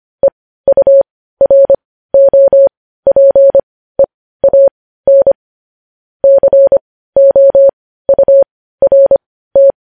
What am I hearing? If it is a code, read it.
Morse code, heard as a keyed tone, about 25 wpm, spaced as for 15 wpm, 550 Hz. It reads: EUROPEAN COURT
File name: podcast / News headlines-1x-25x15